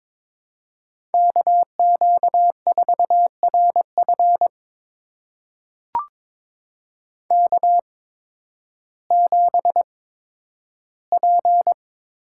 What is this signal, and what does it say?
Morse code, heard as a keyed tone, about 22 wpm, spaced as for 12 wpm, 700 Hz. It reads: KQ4RF K7P
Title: USA Call Signs - 5 Characters - 8x Character Spacing - 22wpm